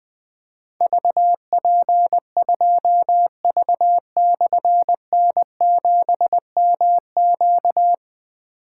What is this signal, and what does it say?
Morse code, heard as a keyed tone, about 20 wpm, 700 Hz. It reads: VP2V/N7MQ
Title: Difficult Call Signs 20wpm